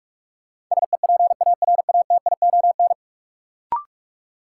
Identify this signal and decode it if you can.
Morse code, heard as a keyed tone, about 48 wpm, 700 Hz. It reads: SEPARATION
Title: Sets of 5 Words - New Word Drill - 45wpm